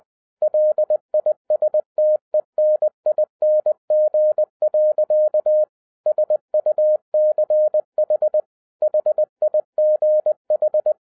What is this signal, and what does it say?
Morse code, heard as a keyed tone, about 20 wpm, 610 Hz. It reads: LISTENING. SUCH HIGH
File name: 2011-10-mp3 / n13-TIPS FROM MY 34 YEARS OF EXPERIENCE 20wpm